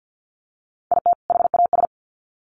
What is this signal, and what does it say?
Morse code, heard as a keyed tone, about 50 wpm, 700 Hz. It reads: IT HAS
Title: Sets of 2 Words 50wpm